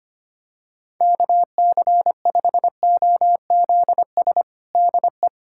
Morse code, heard as a keyed tone, about 25 wpm, 700 Hz. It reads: KC5OZS DE
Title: USA Callsign DE Callsign 25wpm